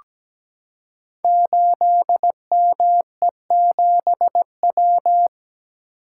Morse code, heard as a keyed tone, about 17 wpm, 700 Hz. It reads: 8ME7W
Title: Code Groups - 5 Letter:Numbers - 17wpm